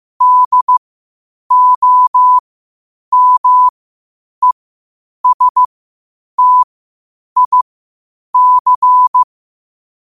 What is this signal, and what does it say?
Morse code, heard as a keyed tone, about 15 wpm, spaced as for 8 wpm, 1000 Hz. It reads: DOMESTIC
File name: qotdfiles / 8wpm